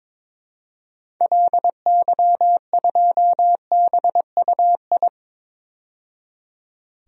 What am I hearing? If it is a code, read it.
Morse code, heard as a keyed tone, about 22 wpm, 700 Hz. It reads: LY2BUI
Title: DXCC-100 Call Signs - 6 Characters - 22wpm